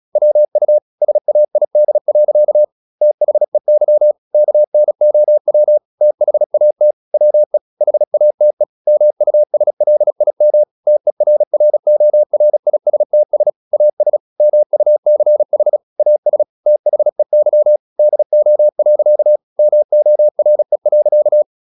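Morse code, heard as a keyed tone, about 36 wpm, 600 Hz. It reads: WU SAID. THEY KNOW THAT WE HATE MUSLIM TERRORISTS AS MUCH AS THEY DO. MORE.
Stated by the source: Morse practice at 36 wpm, text of Ken Follett's novel Never, from CW med Gnister.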